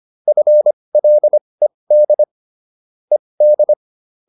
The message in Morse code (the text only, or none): FLED ED